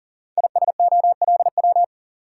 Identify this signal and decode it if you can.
Morse code, heard as a keyed tone, about 39 wpm, 700 Hz. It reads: ISOLW
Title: Code Groups - 5 Letter:Numbers - 40wpm